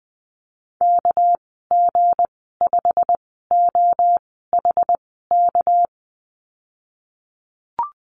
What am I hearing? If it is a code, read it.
Morse code, heard as a keyed tone, about 20 wpm, spaced as for 15 wpm, 700 Hz. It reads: KG5OHK
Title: 200 Lesson 32g - Letter k - Call Signs 20wpm